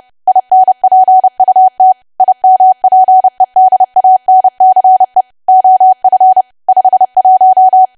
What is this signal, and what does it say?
Morse code, heard as a keyed tone, about 30 wpm, 750 Hz. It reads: INPUT IMPEDANCE OF 51
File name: cwa / W1AW-Level3 / 30WPM